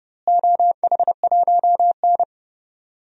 Morse code, heard as a keyed tone, about 30 wpm, 700 Hz. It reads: OH1N
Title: DXCC-100 Call Signs - 4 Characters - Repeated 3x - 30wpm